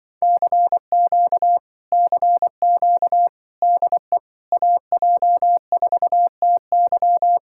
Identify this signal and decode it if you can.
Morse code, heard as a keyed tone, about 25 wpm, 700 Hz. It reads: CQ CQ DE AJ4TY